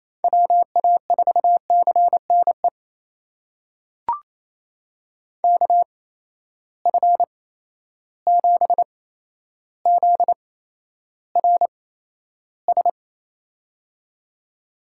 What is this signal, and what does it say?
Morse code, heard as a keyed tone, about 28 wpm, spaced as for 12 wpm, 700 Hz. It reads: WA4CNE KF7ZRS
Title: USA Call Signs - 6 Characters - 8x Character Spacing - 28wpm